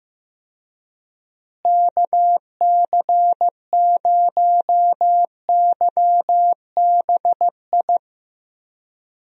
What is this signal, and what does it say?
Morse code, heard as a keyed tone, about 15 wpm, 700 Hz. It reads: KC0YBI
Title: USA Call Signs - 6 Characters - Repeated 2x - 15wpm